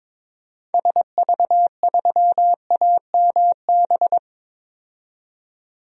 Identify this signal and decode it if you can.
Morse code, heard as a keyed tone, about 22 wpm, 700 Hz. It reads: SV3AMB
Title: DXCC-100 Call Signs - 6 Characters - 22wpm